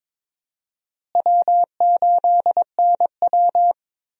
Morse code, heard as a keyed tone, about 22 wpm, 700 Hz. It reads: W8NW
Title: USA Call Signs - 4 Characters - Repeated 3x - 22wpm